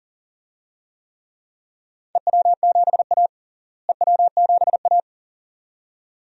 Morse code, heard as a keyed tone, about 38 wpm, 700 Hz. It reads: EW7A EW7A
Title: DXCC-100 Call Signs - 4 Characters - Repeated 2x - 40wpm